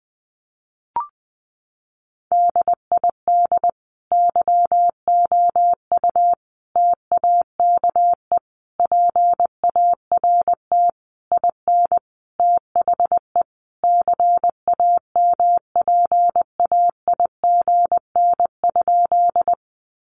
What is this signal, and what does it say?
Morse code, heard as a keyed tone, about 20 wpm, 700 Hz. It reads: DID YOU TAKE PART IN THE CAMPAIGN?
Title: Sentences from Top 1200 Words 20wpm